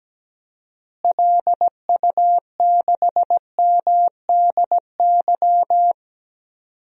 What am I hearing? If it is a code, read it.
Morse code, heard as a keyed tone, about 17 wpm, 700 Hz. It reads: LU6MDY
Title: DXCC-100 Call Signs - 6 Characters - Repeated 3x - 17wpm